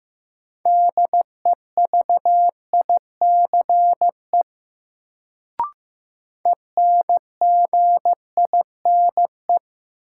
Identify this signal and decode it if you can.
Morse code, heard as a keyed tone, about 15 wpm, 700 Hz. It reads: DEVICE ENGINE
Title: Sentences from Top 1500 Words - New Word Drill Alt - 15wpm